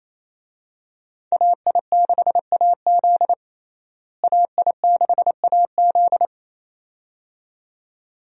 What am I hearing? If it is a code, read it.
Morse code, heard as a keyed tone, about 28 wpm, 700 Hz. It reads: AI6AZ AI6AZ
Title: USA Call Signs - 5 Characters - Repeated 2x - 28wpm